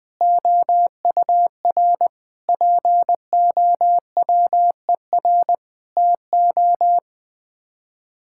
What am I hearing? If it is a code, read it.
Morse code, heard as a keyed tone, about 20 wpm, 700 Hz. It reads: OUR POWER TO
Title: Sets of 8 Words - 20wpm